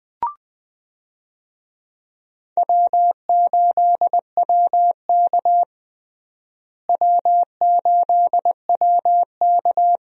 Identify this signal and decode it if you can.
Morse code, heard as a keyed tone, about 21 wpm, 700 Hz. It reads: W8WK W8WK
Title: USA Call Signs - 4 Characters - Repeated 2x - 20wpm